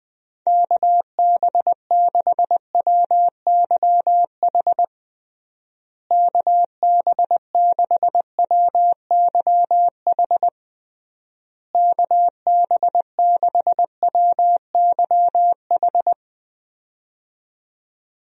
Morse code, heard as a keyed tone, about 20 wpm, 700 Hz. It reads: KB6WYH KB6WYH KB6WYH
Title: USA Call Signs - 6 Characters - Repeated 3x - 20wpm